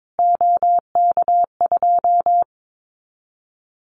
Morse code, heard as a keyed tone, about 22 wpm, 700 Hz. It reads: OK2
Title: Callsign Prefixes for Top 25 DXCC - Repeated 3x - 22wpm